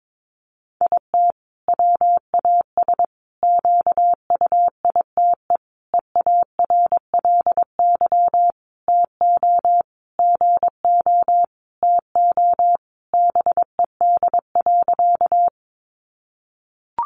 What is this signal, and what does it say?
Morse code, heard as a keyed tone, about 22 wpm, 700 Hz. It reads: IT WAS QUITE EARLY TO GO TO BED.
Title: Sentences from Top 700 Words 22wpm